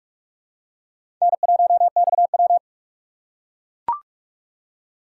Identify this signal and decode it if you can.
Morse code, heard as a keyed tone, about 44 wpm, 700 Hz. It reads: N1XW
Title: USA Call Signs - 4 Characters - 4x Character Spacing - 45wpm